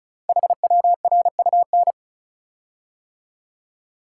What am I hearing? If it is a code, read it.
Morse code, heard as a keyed tone, about 35 wpm, 700 Hz. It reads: HWRUN
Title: Code Groups - 5 Letters - 35wpm